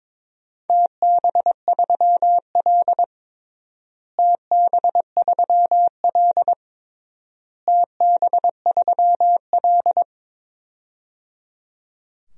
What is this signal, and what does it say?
Morse code, heard as a keyed tone, about 22 wpm, 700 Hz. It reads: TB3L TB3L TB3L
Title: DXCC-100 Call Signs - 4 Characters - Repeated 3x - 22wpm